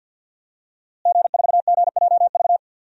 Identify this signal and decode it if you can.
Morse code, heard as a keyed tone, about 51 wpm, 700 Hz. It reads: G4GJV
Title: DXCC-100 Call Signs - 5 Characters - Repeated 3x - 50wpm